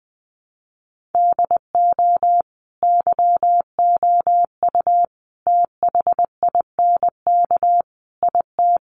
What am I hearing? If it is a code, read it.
Morse code, heard as a keyed tone, about 20 wpm, 700 Hz. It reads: DO YOU THINK IT
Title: Sets of 4 Words 20wpm